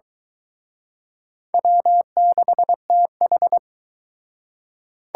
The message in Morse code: W6TH

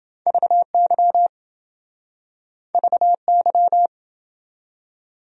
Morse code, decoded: VY VY

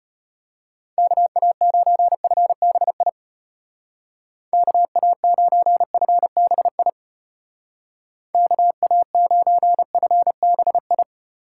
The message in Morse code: KA9FBI KA9FBI KA9FBI

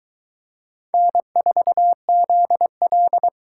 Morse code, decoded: N4ZL

